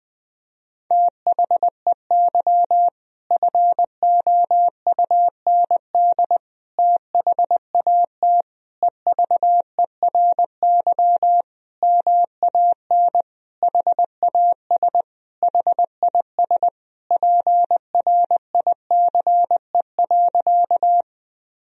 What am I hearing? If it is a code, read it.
Morse code, heard as a keyed tone, about 20 wpm, 700 Hz. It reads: THEY FOUND THAT EVERY MAN HAS HIS PRICE.
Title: Sentences from Top 1000 Words - Review Part 4 - 20wpm